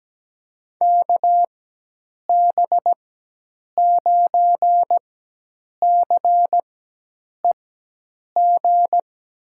Morse code, heard as a keyed tone, about 17 wpm, spaced as for 8 wpm, 700 Hz. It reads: KB9CEG
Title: USA Call Signs - 4x Character Spacing - 17wpm